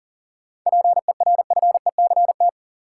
Morse code, heard as a keyed tone, about 40 wpm, 700 Hz. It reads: PERFECT
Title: Sentences from Top 1000 Words - New Word Drill - 40wpm